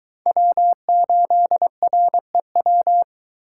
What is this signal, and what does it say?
Morse code, heard as a keyed tone, about 23 wpm, 700 Hz. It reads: W8REW